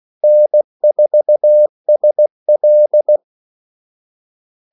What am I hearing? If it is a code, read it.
Morse code, heard as a keyed tone, about 16 wpm, 600 Hz. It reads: N4SL
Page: Kaldesignaler 16 wpm